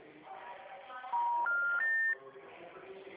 Special information tone, with noise.
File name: out- -2016- - - .32830